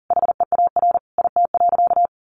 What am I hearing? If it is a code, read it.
Morse code, heard as a keyed tone, about 37 wpm, 700 Hz. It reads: HEAR IT.
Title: Sentences from Top 1500 Words 40wpm